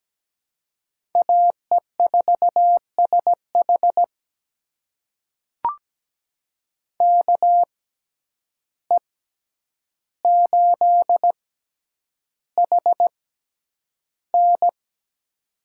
Morse code, heard as a keyed tone, about 17 wpm, spaced as for 9 wpm, 700 Hz. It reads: AE4SH KE8HN
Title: USA Call Signs - 5 Characters - 6x Character Spacing - 17wpm